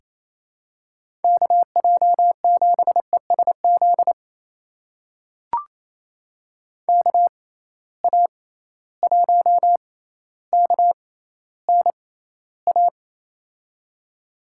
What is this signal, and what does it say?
Morse code, heard as a keyed tone, about 27 wpm, spaced as for 14 wpm, 700 Hz. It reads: KJ7ESZ KA1KNA
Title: USA Call Signs - 6 Characters - 6x Character Spacing - 28wpm